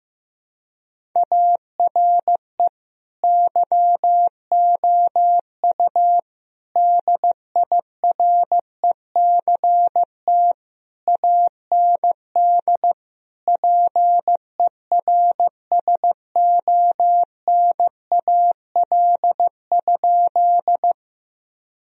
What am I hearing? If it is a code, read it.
Morse code, heard as a keyed tone, about 15 wpm, 700 Hz. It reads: ARE YOU DIRECT AND PERSONAL?
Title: Sentences from Top 1000 Words - Review Part 4 - 15wpm